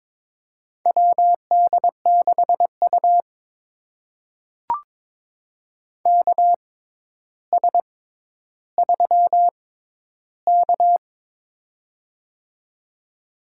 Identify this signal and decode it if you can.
Morse code, heard as a keyed tone, about 22 wpm, spaced as for 11 wpm, 700 Hz. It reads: WD6U KS3K
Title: USA Call Signs - 4 Characters - 6x Character Spacing - 22wpm